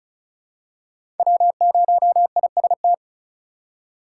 Morse code, heard as a keyed tone, about 36 wpm, 700 Hz. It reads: W0IST
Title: USA Call Signs - 3x Character Spacing - 35wpm